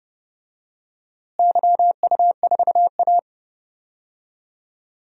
Morse code, heard as a keyed tone, about 29 wpm, 700 Hz. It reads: YU4A
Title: DXCC-100 Call Signs - 4 Characters - Repeated 3x - 30wpm